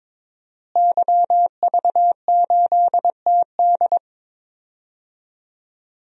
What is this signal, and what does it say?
Morse code, heard as a keyed tone, about 22 wpm, 700 Hz. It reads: YV8TD
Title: DXCC-100 Call Signs - 5 Characters - Repeated 2x - 22wpm